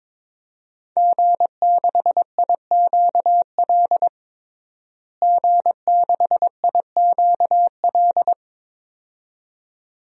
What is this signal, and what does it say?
Morse code, heard as a keyed tone, about 22 wpm, 700 Hz. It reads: G6IQL G6IQL
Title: DXCC-100 Call Signs - 5 Characters - Repeated 2x - 22wpm